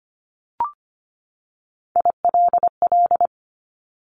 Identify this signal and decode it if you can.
Morse code, heard as a keyed tone, about 25 wpm, 700 Hz. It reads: ILL
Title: Top 1751-2000 Words 25wpm